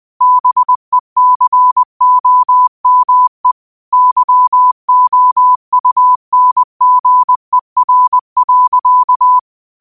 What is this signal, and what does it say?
Morse code, heard as a keyed tone, about 20 wpm, 1000 Hz. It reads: BECOME YOUNGER.
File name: qotdfiles / 20wpm